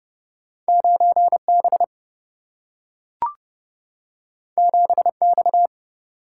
Morse code, heard as a keyed tone, about 31 wpm, 700 Hz. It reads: 9 B 7 X
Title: Single-Letter Single-Number 30wpm